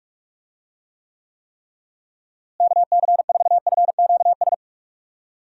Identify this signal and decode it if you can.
Morse code, heard as a keyed tone, about 42 wpm, 700 Hz. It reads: KC4FQS
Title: USA Call Signs - 6 Characters - Repeated 3x - 45wpm